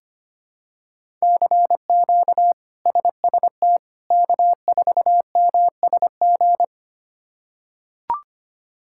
Morse code, heard as a keyed tone, about 25 wpm, 700 Hz. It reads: CQ SST K4MSG
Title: K1USN SST QSO Practice - 25wpm